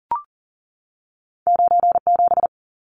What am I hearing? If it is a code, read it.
Morse code, heard as a keyed tone, about 40 wpm, 700 Hz. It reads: 97
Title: Numbers - 2-Digit 40wpm